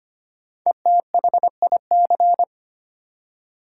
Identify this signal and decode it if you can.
Morse code, heard as a keyed tone, about 25 wpm, 700 Hz. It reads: ETHIC
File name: 5 Letter Words - First 3 Letters Given - 25wpm